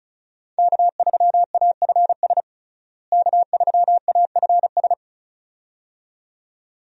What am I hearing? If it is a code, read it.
Morse code, heard as a keyed tone, about 34 wpm, 700 Hz. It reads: K3AFS K3AFS
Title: USA Call Signs - 5 Characters - Repeated 2x - 35wpm